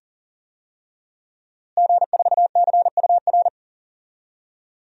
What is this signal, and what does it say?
Morse code, heard as a keyed tone, about 39 wpm, 700 Hz. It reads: G4CUR
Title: DXCC-100 Call Signs - 5 Characters - Repeated 2x - 40wpm